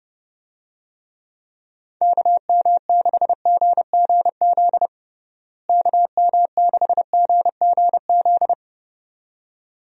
Morse code, heard as a keyed tone, about 32 wpm, 700 Hz. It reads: KM6GGZ KM6GGZ
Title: USA Call Signs - 6 Characters - Repeated 2x - 30wpm